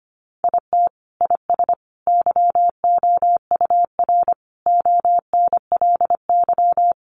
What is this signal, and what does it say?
Morse code, heard as a keyed tone, about 25 wpm, 700 Hz. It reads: IT IS YOUR ONLY
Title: Sentences from Top 1000 Words 25wpm